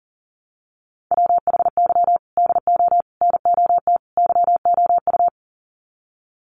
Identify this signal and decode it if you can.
Morse code, heard as a keyed tone, about 40 wpm, 700 Hz. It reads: WHY DO NOT YOU
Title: Sets of 4 Words 40wpm